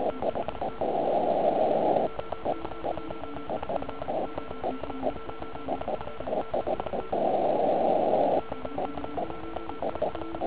Morse code, heard as a keyed tone, about 36 wpm, 1340 Hz, with noise. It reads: TEST OK1IF OK1IF TEST OK1IF
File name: Archiv / CUC149 / OK1IF CUC149